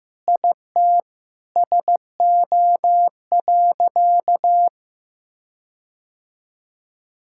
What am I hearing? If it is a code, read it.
Morse code, heard as a keyed tone, about 15 wpm, 700 Hz. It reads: IT SO.
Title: Sentences from Top 1600 Words 15wpm